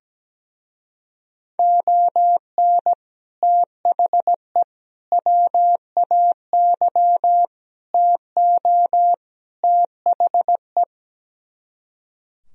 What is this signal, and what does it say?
Morse code, heard as a keyed tone, about 17 wpm, 700 Hz. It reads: ON THE WAY TO THE
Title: Speed-Racing - Sets of 5 Words 17wpm